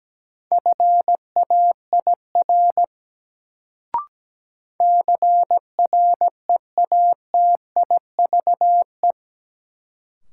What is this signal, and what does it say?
Morse code, heard as a keyed tone, about 17 wpm, 700 Hz. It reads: FAIR CREATIVE
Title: Sentences from Top 1000 Words - Review Part 5 - New Word Drill - 17wpm